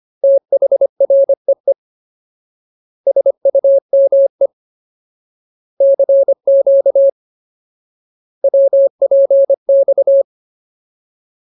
Morse code, heard as a keyed tone, about 25 wpm, 550 Hz. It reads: THREE SUME CQ WPX
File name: Ultimate Word List / 25WPM Ultimate